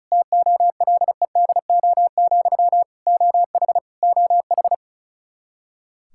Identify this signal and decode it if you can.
Morse code, heard as a keyed tone, about 36 wpm, 700 Hz. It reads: TOLEDO, OH OH
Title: Practice QSO - Exchange 1 - Beginning - 35wpm